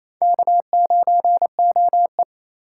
Morse code, heard as a keyed tone, about 36 wpm, 700 Hz. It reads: K9OE